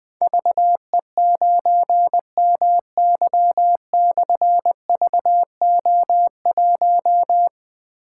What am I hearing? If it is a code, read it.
Morse code, heard as a keyed tone, about 20 wpm, 700 Hz. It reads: VE9MY/VO1
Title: Difficult Call Signs 20wpm